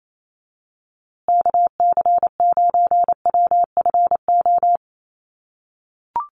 Morse code, heard as a keyed tone, about 28 wpm, 700 Hz. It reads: KC9WFO
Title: USA Call Signs Repeated 3x 28wpm